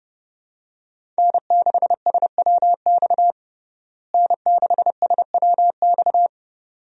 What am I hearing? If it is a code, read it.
Morse code, heard as a keyed tone, about 30 wpm, 700 Hz. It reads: N6SWX N6SWX
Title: USA Call Signs - 5 Characters - Repeated 2x - 30wpm